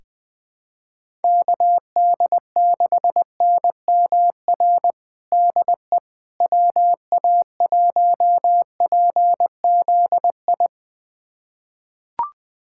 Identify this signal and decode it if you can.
Morse code, heard as a keyed tone, about 20 wpm, 700 Hz. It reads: KD6NMR DE WA1PZI
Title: USA Callsign DE Callsign 20wpm